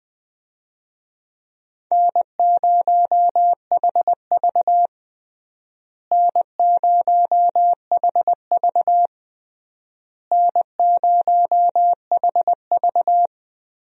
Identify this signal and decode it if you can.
Morse code, heard as a keyed tone, about 20 wpm, 700 Hz. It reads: N0HV N0HV N0HV
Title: USA Call Signs - 4 Characters - Repeated 3x - 20wpm